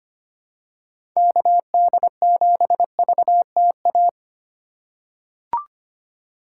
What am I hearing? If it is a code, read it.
Morse code, heard as a keyed tone, about 25 wpm, 700 Hz. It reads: KD7VTA
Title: USA Call Signs - 6 Characters - 4x Character Spacing - 25wpm